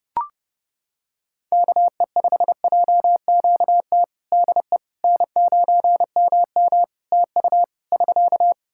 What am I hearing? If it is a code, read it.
Morse code, heard as a keyed tone, about 30 wpm, 700 Hz. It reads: KE5JQT DE N9MM TU <SK>
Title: K1USN SST QSO Practice 30wpm